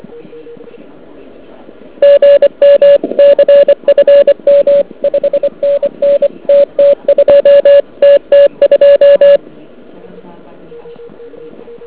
Morse code, heard as a keyed tone, about 25 wpm, 570 Hz. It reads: GM CFM 5NN TT2 TT2
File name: ARCHIV / KX3 / KV PA